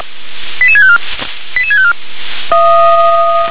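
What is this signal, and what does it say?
Five-tone selective call (ZVEI), heard as quick runs of five tones, tones of about 70 ms. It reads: 88654 88654
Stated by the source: ZVEI five-tone selective call received on SDR.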